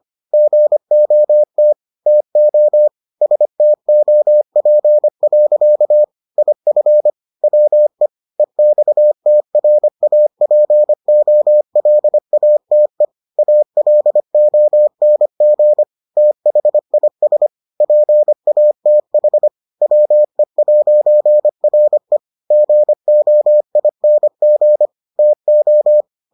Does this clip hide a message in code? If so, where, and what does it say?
Yes: Morse code from 0.3 to 26.0 s, heard as a keyed tone, about 25 wpm, 600 Hz. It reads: GOT TO STOP. IF WE EXTRAPOLATE ALONG THIS PATH WE'RE GOING TO